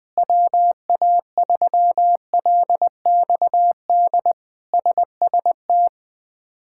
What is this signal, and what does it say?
Morse code, heard as a keyed tone, about 21 wpm, 700 Hz. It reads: WA3LXD SST
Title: K1USN SST QSO Practice - 20wpm